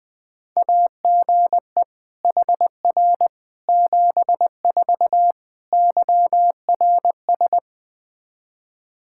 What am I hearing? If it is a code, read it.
Morse code, heard as a keyed tone, about 20 wpm, 700 Hz. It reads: AGE HR 74 YRS
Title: Practice QSO - Exchange 3 - More Info - 20wpm